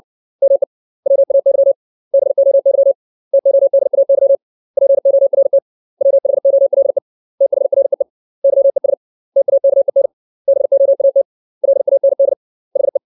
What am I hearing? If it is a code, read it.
Morse code, heard as a keyed tone, about 60 wpm, 550 Hz. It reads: ME WAY BOY TODAY PORT WHOLE THREE YES TAKEN BOAT LAND HE
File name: 632 words / 632 Words 60WPM